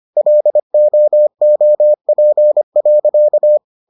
Morse code, heard as a keyed tone, about 25 wpm, 600 Hz. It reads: LOOP.